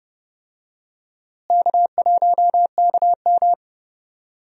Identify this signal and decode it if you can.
Morse code, heard as a keyed tone, about 30 wpm, 700 Hz. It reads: K1KM
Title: USA Call Signs - 4 Characters - Repeated 2x - 30wpm